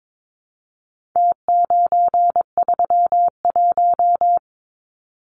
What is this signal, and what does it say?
Morse code, heard as a keyed tone, about 22 wpm, 700 Hz. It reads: T931
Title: Numbers - Contest Style - 4-Digit 22wpm